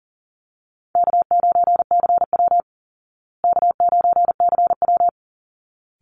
Morse code, heard as a keyed tone, about 40 wpm, 700 Hz. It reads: K9CW K9CW
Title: Easy Call Signs Repeated 2x - 40wpm